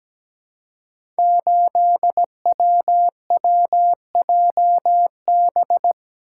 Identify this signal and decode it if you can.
Morse code, heard as a keyed tone, about 17 wpm, 700 Hz. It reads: 8WWJB
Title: Code Groups - 5 Letter:Numbers - 17wpm